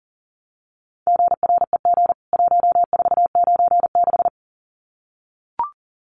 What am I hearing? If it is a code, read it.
Morse code, heard as a keyed tone, about 40 wpm, 700 Hz. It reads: GREG 1496
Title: CWT Exchange 40wpm